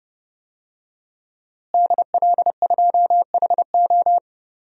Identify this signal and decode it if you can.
Morse code, heard as a keyed tone, about 30 wpm, 700 Hz. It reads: DL2HO
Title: DXCC-100 Call Signs - 5 Characters - Repeated 2x - 30wpm